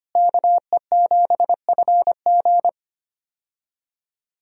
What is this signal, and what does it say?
Morse code, heard as a keyed tone, about 25 wpm, 700 Hz. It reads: KE7FG
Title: USA Call Signs - 5 Characters - Repeated 2x - 25wpm